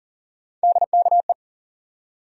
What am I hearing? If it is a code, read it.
Morse code, heard as a keyed tone, about 40 wpm, 700 Hz. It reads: DKE
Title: USA Call Sign Postfixes - 3 Letters 40wpm